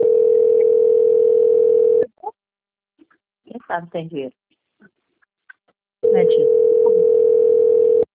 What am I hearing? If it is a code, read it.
Ringback tone (North America), 440 + 480 Hz, with speech.
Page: Americana - SP